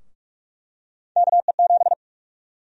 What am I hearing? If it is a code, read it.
Morse code, heard as a keyed tone, about 46 wpm, 700 Hz. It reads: KE7
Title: USA Call Sign Prefixes - 45wpm